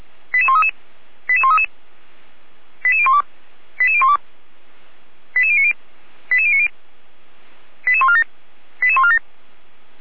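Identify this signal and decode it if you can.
Five-tone selective call (ZVEI), heard as quick runs of five tones, tones of about 80 ms. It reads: 88133 88133 88012 88012 88090 88090 88147 88147
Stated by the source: ZVEI five-tone selective call 88149 received by SDR, from Feuerwehr Orlamünde audio server.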